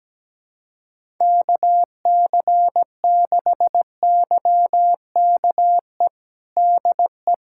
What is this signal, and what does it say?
Morse code, heard as a keyed tone, about 17 wpm, 700 Hz. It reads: KC6YKE DE
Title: USA Callsign DE Callsign 17wpm